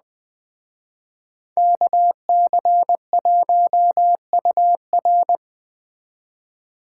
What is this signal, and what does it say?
Morse code, heard as a keyed tone, about 20 wpm, 700 Hz. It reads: KC1UR
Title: Speed Racing - USA Call Signs 17wpm